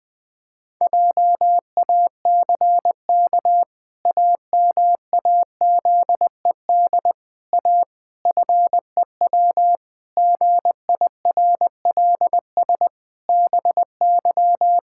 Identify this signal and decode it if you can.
Morse code, heard as a keyed tone, about 20 wpm, 700 Hz. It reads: JACK AMAZED A FEW GIRLS BY
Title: Pangram Challenge v2 - 20wpm